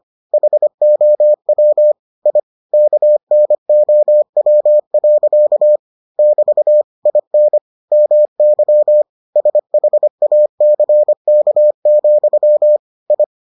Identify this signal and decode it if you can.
Morse code, heard as a keyed tone, about 25 wpm, 600 Hz. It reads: HOW I KNOW. <BT> IN MY SHACK, I